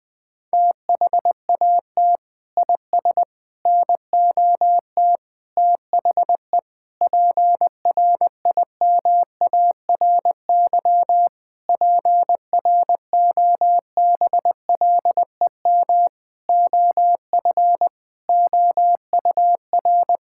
Morse code, heard as a keyed tone, about 20 wpm, 700 Hz. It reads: THAT IS NOT THE PRIMARY PROBLEM OF OUR
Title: Sentences from Top 1000 Words - Review Part 5 - 20wpm